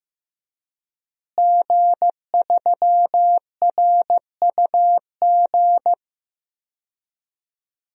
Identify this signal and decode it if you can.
Morse code, heard as a keyed tone, about 15 wpm, 700 Hz. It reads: G3RUG
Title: DXCC-100 Call Signs - 5 Characters - Repeated 2x - 15wpm